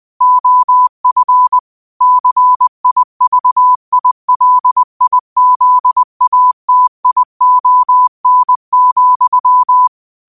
Morse code, heard as a keyed tone, about 20 wpm, 1000 Hz. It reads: OF CIVILIZATION,
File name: qotdfiles / 20wpm